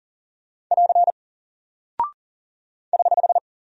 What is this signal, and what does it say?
Morse code, heard as a keyed tone, about 40 wpm, 700 Hz. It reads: <AR> <HH>
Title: Single Character (Letter:Number:Punctuation:Prosign) 40wpm